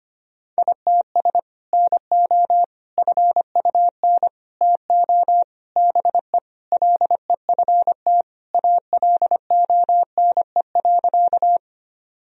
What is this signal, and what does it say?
Morse code, heard as a keyed tone, about 25 wpm, 700 Hz. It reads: ITS NO FUN TO BE LEFT ALONE.
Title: Sentences from Top 1000 Words - Review Part 4 - 25wpm